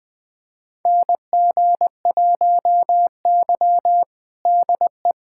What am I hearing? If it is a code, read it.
Morse code, heard as a keyed tone, about 20 wpm, 700 Hz. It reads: NG1Y DE
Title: USA Callsign DE Callsign - 4 Characters - 20wpm